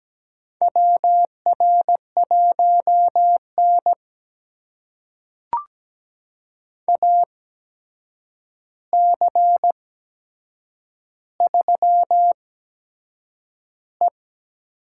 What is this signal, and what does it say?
Morse code, heard as a keyed tone, about 17 wpm, spaced as for 7 wpm, 700 Hz. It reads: WR1N AC3E
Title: USA Call Signs - 4 Characters - 8x Character Spacing - 17wpm